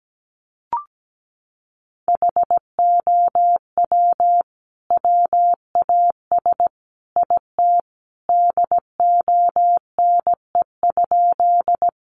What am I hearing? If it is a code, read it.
Morse code, heard as a keyed tone, about 17 wpm, 700 Hz. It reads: HOW WAS IT DONE?
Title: Sentences from Top 500 Words 17wpm